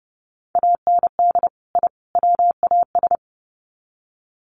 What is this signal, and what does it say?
Morse code, heard as a keyed tone, about 30 wpm, 700 Hz. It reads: AND I WAS
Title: Sets of 3 Words 30wpm